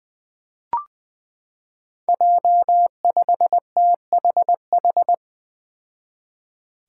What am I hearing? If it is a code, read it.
Morse code, heard as a keyed tone, about 21 wpm, 700 Hz. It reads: J5THH
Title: Five Challenging Letter-Number Combinations 20wpm